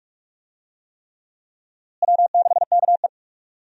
Morse code, heard as a keyed tone, about 41 wpm, 700 Hz. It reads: W6KE
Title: USA Call Signs - 4 Characters - Repeated 3x - 45wpm